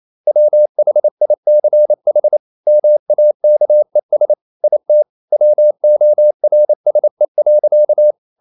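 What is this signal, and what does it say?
Morse code, heard as a keyed tone, about 28 wpm, 600 Hz. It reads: WHICH MAKES IT WORSE.
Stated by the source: Morse code practice at 28 wpm, text of Ken Follett's novel Never in English, from CW med Gnister.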